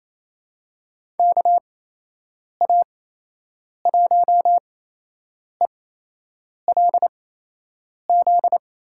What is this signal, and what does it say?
Morse code, heard as a keyed tone, about 27 wpm, spaced as for 8 wpm, 700 Hz. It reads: KA1ELZ
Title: USA Call Signs - 6 Characters - 8x Character Spacing - 28wpm